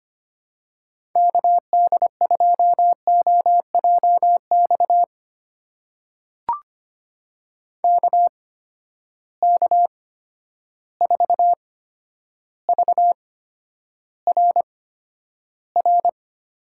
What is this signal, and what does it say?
Morse code, heard as a keyed tone, about 25 wpm, spaced as for 11 wpm, 700 Hz. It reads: KD2OJX KK4VRR
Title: USA Call Signs - 6 Characters - 8x Character Spacing - 25wpm